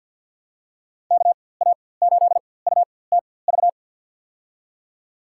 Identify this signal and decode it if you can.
Morse code, heard as a keyed tone, about 53 wpm, spaced as for 24 wpm, 700 Hz. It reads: KA8UTV
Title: USA Call Signs - 6 Characters - 4x Character Spacing - 50wpm